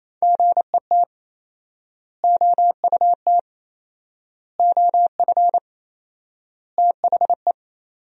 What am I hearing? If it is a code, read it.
Morse code, heard as a keyed tone, about 28 wpm, 700 Hz. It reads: GET OUT OF THE
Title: 4x Word-Spacing - Sentences from Top 200 Words - 28wpm